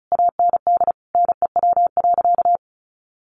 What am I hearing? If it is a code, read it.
Morse code, heard as a keyed tone, about 34 wpm, 700 Hz. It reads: AND NEW.
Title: Sentences from Top 1500 Words 35wpm